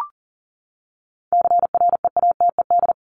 Morse code, heard as a keyed tone, about 40 wpm, 700 Hz. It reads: CREATED
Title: ED Suffix - Top 100 Words with Base Words 40wpm